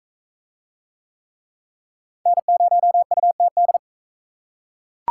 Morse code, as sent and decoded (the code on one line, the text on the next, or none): -. ----- ..- - -..
N0UTD